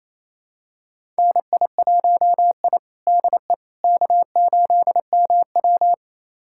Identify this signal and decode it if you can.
Morse code, heard as a keyed tone, about 28 wpm, 700 Hz. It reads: NI1I DE K8MW
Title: USA Callsign DE Callsign - 4 Characters - 28wpm